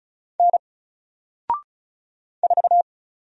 Morse code, heard as a keyed tone, about 36 wpm, 700 Hz. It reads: N 4
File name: Single-Letter-Number-35wpm